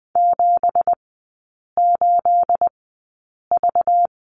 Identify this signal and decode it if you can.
Morse code, heard as a keyed tone, about 20 wpm, spaced as for 8 wpm, 700 Hz. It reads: 78V
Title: 212 Lesson 34e - Number 8 - Three Characters 20wpm